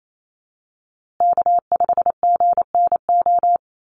Morse code, heard as a keyed tone, about 28 wpm, 700 Hz. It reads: K5GNO
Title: USA Call Signs Repeated 2x 28wpm